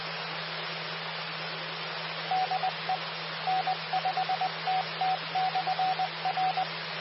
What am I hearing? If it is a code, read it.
Morse code, heard as a keyed tone, about 20 wpm, 750 Hz. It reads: DE N5TT/R
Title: Austin, TX Amateur Repeaters